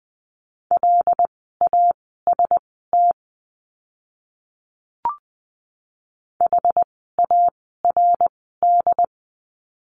Morse code, heard as a keyed tone, about 20 wpm, spaced as for 14 wpm, 700 Hz. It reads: LAST HARD
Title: 075 Lesson 13f - Letter L - Words 20wpm